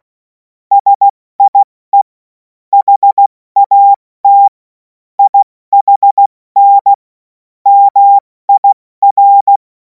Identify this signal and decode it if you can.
Morse code, heard as a keyed tone, about 16 wpm, 800 Hz. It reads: SIE HAT IHN MIR
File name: Morsekurs / Witze / Witz 07. April 2026 70 F80